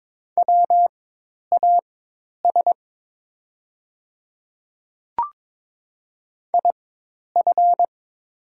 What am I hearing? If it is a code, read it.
Morse code, heard as a keyed tone, about 22 wpm, spaced as for 10 wpm, 700 Hz. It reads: WAS IF
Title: 4x Character Spacing - Top 100 Words - 22wpm